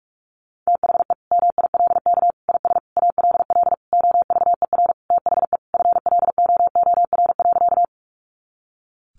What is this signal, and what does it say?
Morse code, heard as a keyed tone, about 44 wpm, 700 Hz. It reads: THE MILK IS ALL OVER THE FLOOR.
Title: Additional Easy Phrases 45wpm